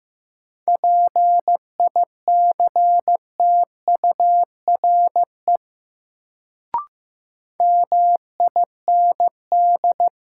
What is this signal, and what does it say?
Morse code, heard as a keyed tone, about 15 wpm, 700 Hz. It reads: PICTURE MIND
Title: Sentences from Top 500 Words - New Word Drill - 15wpm